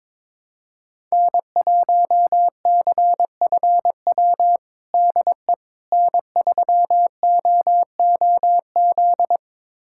Morse code, heard as a keyed tone, about 22 wpm, 700 Hz. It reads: N1CFW DE N3OOZ
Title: USA Callsign DE Callsign 22wpm